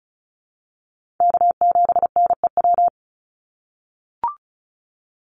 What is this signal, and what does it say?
Morse code, heard as a keyed tone, about 34 wpm, 700 Hz. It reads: K7NEW
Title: NA SOTA Chaser Callsigns 35wpm